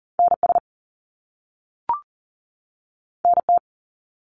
Morse code, heard as a keyed tone, about 41 wpm, 700 Hz. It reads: NS NT
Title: Canadian Provinces and Territories 40wpm